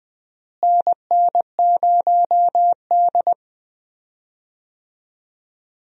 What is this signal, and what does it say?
Morse code, heard as a keyed tone, about 20 wpm, 700 Hz. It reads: NN0D
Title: USA Call Signs - 4 Characters - Repeated 2x - 20wpm